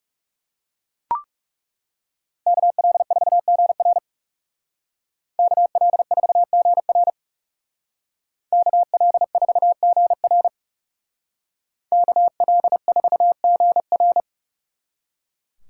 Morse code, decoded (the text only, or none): KL4GR KL4GR KL4GR KL4GR